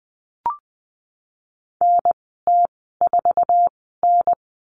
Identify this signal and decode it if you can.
Morse code, heard as a keyed tone, about 20 wpm, spaced as for 15 wpm, 700 Hz. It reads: NT4N
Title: 048 Lesson 9g - Number 4 - Call Signs 20wpm